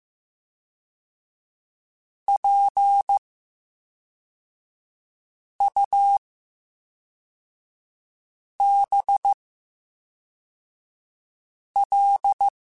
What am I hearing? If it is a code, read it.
Morse code, heard as a keyed tone, about 15 wpm, spaced as for 3 wpm, 800 Hz. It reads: PUBL
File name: Special INGs / Special INGs-mp3s / VerySlowToFaster Ing @10